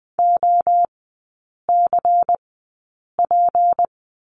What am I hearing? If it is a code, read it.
Morse code, heard as a keyed tone, about 20 wpm, spaced as for 8 wpm, 690 Hz. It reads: OCP
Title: 159 Lesson 26e - Letter g - Three Characters 20wpm